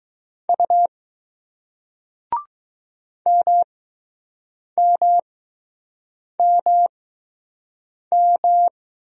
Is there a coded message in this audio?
Yes: Morse code from 0.5 to 8.7 s, heard as a keyed tone, about 20 wpm, spaced as for 6 wpm, 700 Hz. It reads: U MMMM